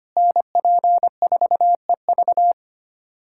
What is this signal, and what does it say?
Morse code, heard as a keyed tone, about 25 wpm, 700 Hz. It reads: NP4EV
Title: USA Callsign DE Callsign 25wpm